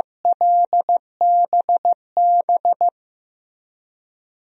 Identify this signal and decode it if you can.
Morse code, heard as a keyed tone, about 16 wpm, 700 Hz. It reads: LBB
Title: DXCC-100 Call Signs - 6 Characters - Repeated 2x - 15wpm